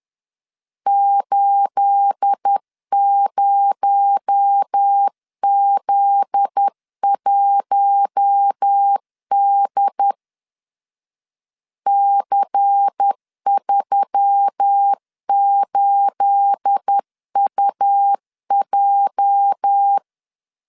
Morse code, heard as a keyed tone, about 10 wpm, 800 Hz. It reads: 80Z1D C38UJ
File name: CW / DET / 1 9 1500C S B 10wpm 4439